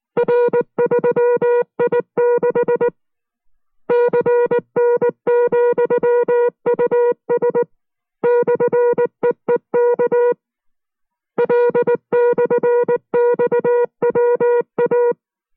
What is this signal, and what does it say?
Morse code, heard as a keyed tone, about 19 wpm, 480 Hz. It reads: R3I6 CN,US /EEK L/XWA